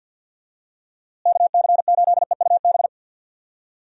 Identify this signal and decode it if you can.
Morse code, heard as a keyed tone, about 50 wpm, 700 Hz. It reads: KC8EUB
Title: USA Call Signs - 6 Characters - Repeated 2x - 50wpm